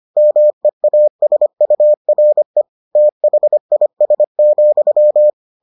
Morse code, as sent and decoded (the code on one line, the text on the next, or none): -- . .- ... ..- .-. / - .... .. ... --..--
MEASURE THIS,